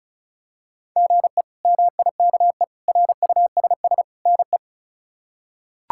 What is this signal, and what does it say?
Morse code, heard as a keyed tone, about 34 wpm, 700 Hz. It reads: GE MIKE RUSS NE
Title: K1USN SST QSO Practice - 35wpm